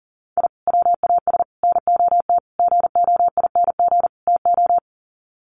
Morse code, heard as a keyed tone, about 40 wpm, 700 Hz. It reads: I WAS NOT GOING TO
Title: Sets of 5 Words 40wpm